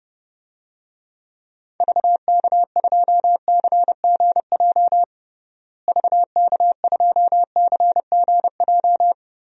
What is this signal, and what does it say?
Morse code, heard as a keyed tone, about 30 wpm, 700 Hz. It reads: VK2CGJ VK2CGJ
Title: DXCC-100 Call Signs - 6 Characters - Repeated 3x - 30wpm